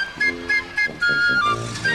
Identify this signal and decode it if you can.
There is no code.